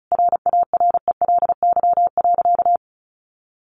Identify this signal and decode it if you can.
Morse code, heard as a keyed tone, about 34 wpm, 700 Hz. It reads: RARELY.
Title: Sentences from Top 1400 Words 35wpm